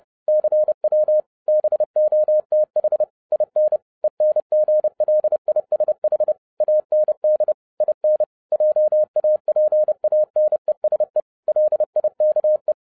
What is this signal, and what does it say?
Morse code, heard as a keyed tone, about 30 wpm, 610 Hz. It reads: CW BOTH IN ENGLISH AND IN JAPANESE LIKE